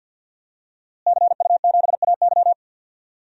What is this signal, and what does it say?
Morse code, heard as a keyed tone, about 50 wpm, 700 Hz. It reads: CU7AY
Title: DXCC-100 Call Signs - 5 Characters - Repeated 3x - 50wpm